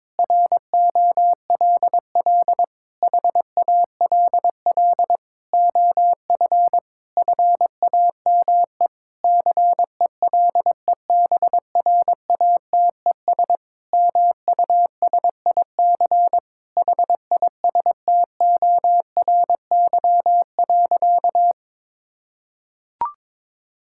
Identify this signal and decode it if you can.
Morse code, heard as a keyed tone, about 22 wpm, 700 Hz. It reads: ROLL HALL OF FAME CELEBRATES MUSIC HISTORY.
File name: Sentences with 7-Word Phrases - 22wpm